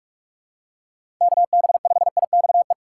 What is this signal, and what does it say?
Morse code, heard as a keyed tone, about 46 wpm, 700 Hz. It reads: KB5IXE
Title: USA Call Signs - 6 Characters - 3x Character Spacing - 45wpm